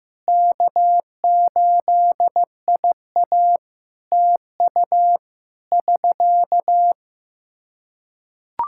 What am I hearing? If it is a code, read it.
Morse code, heard as a keyed tone, about 15 wpm, 700 Hz. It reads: K8IA TU <SK>
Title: K1USN SST QSO Practice 15wpm